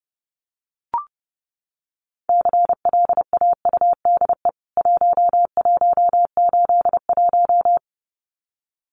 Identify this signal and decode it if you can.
Morse code, heard as a keyed tone, about 28 wpm, 700 Hz. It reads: CLAUDE 1181
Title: CWT Exchange 30wpm